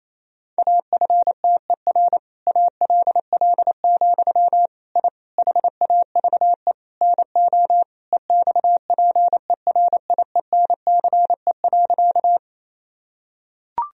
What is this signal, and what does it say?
Morse code, heard as a keyed tone, about 28 wpm, 700 Hz. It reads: AFTER ALL, I HAVE NO EXPERIENCE.
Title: Sentences from Top 1000 Words - Review Part 3 - 28wpm